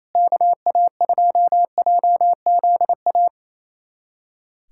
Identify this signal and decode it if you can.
Morse code, heard as a keyed tone, about 28 wpm, 700 Hz. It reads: KA2JZA
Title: USA Callsign DE Callsign 28wpm